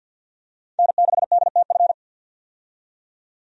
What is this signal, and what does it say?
Morse code, heard as a keyed tone, about 50 wpm, 700 Hz. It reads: N6DTF
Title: USA Call Signs - 5 Characters - 50wpm